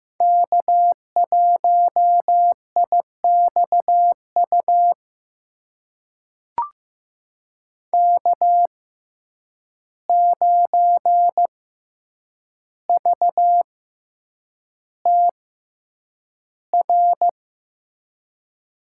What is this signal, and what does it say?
Morse code, heard as a keyed tone, about 15 wpm, spaced as for 8 wpm, 700 Hz. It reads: K1IXU K9VTR
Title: USA Call Signs - 5 Characters - 6x Character Spacing - 15wpm